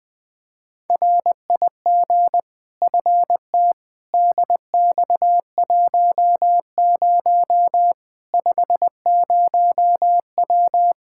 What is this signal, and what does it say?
Morse code, heard as a keyed tone, about 20 wpm, 700 Hz. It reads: RIG FT DX10 50W
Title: K1USN SST QSO Practice 20wpm